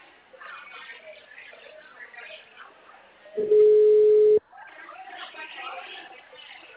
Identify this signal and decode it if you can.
Ringback tone (Europe, 425 Hz), with speech.
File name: out- -2022- - - .157